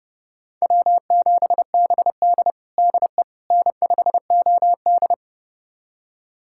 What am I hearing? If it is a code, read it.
Morse code, heard as a keyed tone, about 30 wpm, 700 Hz. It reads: W7BD DE N5OD
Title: USA Callsign DE Callsign - 4 Characters - 30wpm